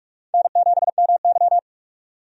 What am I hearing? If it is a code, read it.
Morse code, heard as a keyed tone, about 46 wpm, 700 Hz. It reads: N7MY
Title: USA Call Signs - 4 Characters - 45wpm